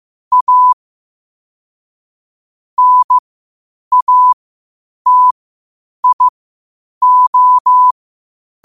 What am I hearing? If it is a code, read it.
Morse code, heard as a keyed tone, about 15 wpm, spaced as for 8 wpm, 1000 Hz. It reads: A NATIO
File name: qotdfiles / 8wpm